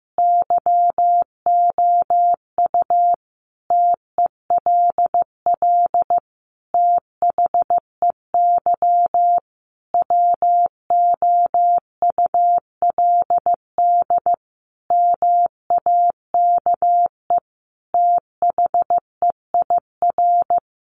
Morse code, heard as a keyed tone, about 15 wpm, 700 Hz. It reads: YOU TELL THEY WOULD MAKE THEIR
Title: Sentences from Top 1000 Words 15wpm